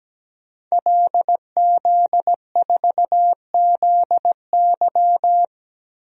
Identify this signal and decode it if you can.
Morse code, heard as a keyed tone, about 17 wpm, 700 Hz. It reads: LZ4ZY
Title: DXCC-100 Call Signs - 5 Characters - 17wpm